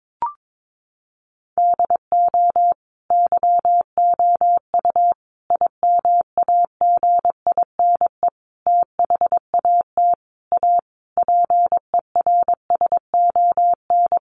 Morse code, heard as a keyed tone, about 22 wpm, 700 Hz. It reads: DO YOU IMAGINE THAT A PERSON
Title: Sentences from Top 1400 Words 22wpm